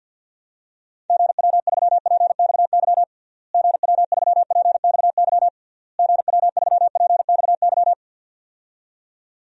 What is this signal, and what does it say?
Morse code, heard as a keyed tone, about 51 wpm, 700 Hz. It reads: GW3PXY GW3PXY GW3PXY
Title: DXCC-100 Call Signs - 6 Characters - Repeated 3x - 50wpm